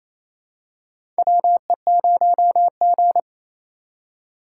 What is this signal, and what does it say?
Morse code, heard as a keyed tone, about 28 wpm, 700 Hz. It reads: WE0G
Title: USA Call Signs - 4x Character Spacing - 28wpm